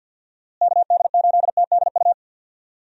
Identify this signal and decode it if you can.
Morse code, heard as a keyed tone, about 47 wpm, 700 Hz. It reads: KD8TDU
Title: USA Call Signs - 6 Characters - Repeated 3x - 50wpm